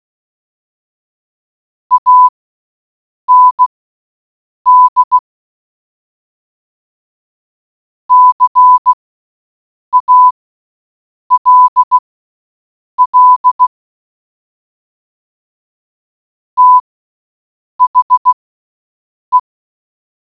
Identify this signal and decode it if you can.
Morse code, heard as a keyed tone, about 16 wpm, spaced as for 7 wpm, 1000 Hz. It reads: AND CALL THE